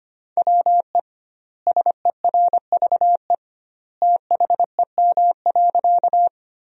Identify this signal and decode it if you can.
Morse code, heard as a keyed tone, about 25 wpm, 700 Hz. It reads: WE SERVE THEM.
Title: 2x Word-Spacing - Sentences from Top 400 Words - 25wpm